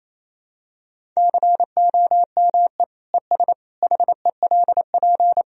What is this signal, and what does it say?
Morse code, heard as a keyed tone, about 28 wpm, 700 Hz. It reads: COME ES HELP
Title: Sentences from Top 200 Words and Abbreviations 28wpm